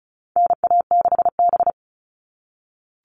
Morse code, heard as a keyed tone, about 34 wpm, 700 Hz. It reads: NA6B
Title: All Cut Numbers - 4-Digit 35wpm